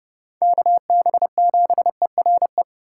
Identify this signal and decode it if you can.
Morse code, heard as a keyed tone, about 29 wpm, 700 Hz. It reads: KB7ERE